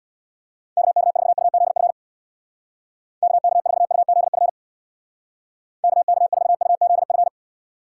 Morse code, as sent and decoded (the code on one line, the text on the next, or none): -.- -.- ....- ..- --.. ..-. / -.- -.- ....- ..- --.. ..-. / -.- -.- ....- ..- --.. ..-.
KK4UZF KK4UZF KK4UZF